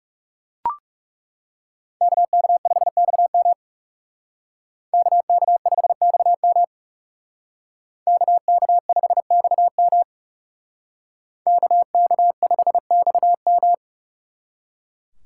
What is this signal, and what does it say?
Morse code, heard as a keyed tone, about 38 wpm, 700 Hz. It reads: KK5XM KK5XM KK5XM KK5XM